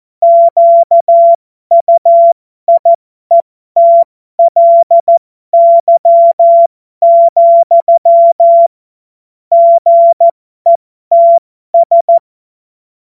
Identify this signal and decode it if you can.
Morse code, heard as a keyed tone, about 14 wpm, spaced as for 12 wpm, 680 Hz. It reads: QUIETLY, GETS
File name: Morsekurs / Witze